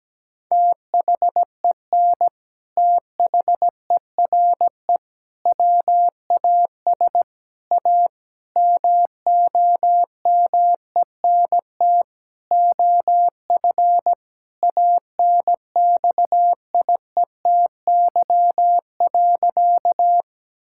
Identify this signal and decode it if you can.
Morse code, heard as a keyed tone, about 17 wpm, 700 Hz. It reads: THEN THERE WAS A MOMENT OF ANXIETY.
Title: Sentences from Top 1600 Words 17wpm